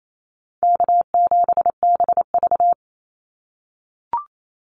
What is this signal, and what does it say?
Morse code, heard as a keyed tone, about 28 wpm, 700 Hz. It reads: K7BV
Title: Easy Call Signs - 28wpm